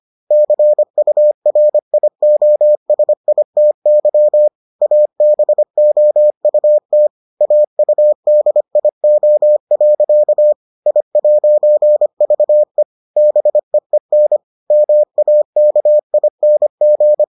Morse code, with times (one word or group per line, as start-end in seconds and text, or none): 0.3-4.5 CURIOSITY
4.8-7.1 ABOUT
7.4-10.5 AUDIO.
10.9-12.8 I'VE
13.2-14.4 BEEN
14.7-17.2 MAKING